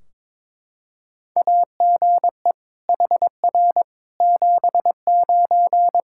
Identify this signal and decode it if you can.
Morse code, heard as a keyed tone, about 21 wpm, 700 Hz. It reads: AGE HR 79
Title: K1USN SST QSO Practice 22wpm